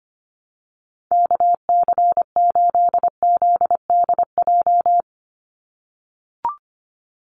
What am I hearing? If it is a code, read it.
Morse code, heard as a keyed tone, about 25 wpm, 700 Hz. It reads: KC8ZDJ
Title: USA Call Signs 25wpm